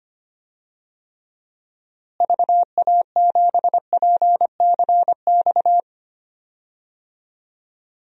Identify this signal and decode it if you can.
Morse code, heard as a keyed tone, about 25 wpm, 700 Hz. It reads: VA7PCX
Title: DXCC-100 Call Signs - 6 Characters - Repeated 3x - 25wpm